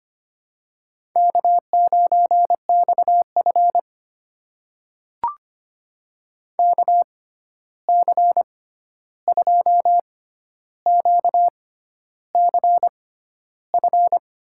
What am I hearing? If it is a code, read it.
Morse code, heard as a keyed tone, about 25 wpm, spaced as for 11 wpm, 700 Hz. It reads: K9XF KC2QCF
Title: USA Call Signs - 6x Character Spacing - 25wpm